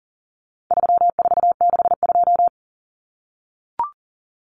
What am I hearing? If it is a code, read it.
Morse code, heard as a keyed tone, about 40 wpm, 700 Hz. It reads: 3462
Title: Common Cut Numbers - 4-Digit 40wpm